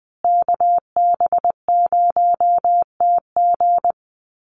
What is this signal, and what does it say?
Morse code, heard as a keyed tone, about 20 wpm, 700 Hz. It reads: KB0TG
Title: USA Call Signs Repeated 3x 20wpm